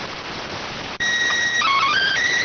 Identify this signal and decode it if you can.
SSTV (slow-scan television), PD 290.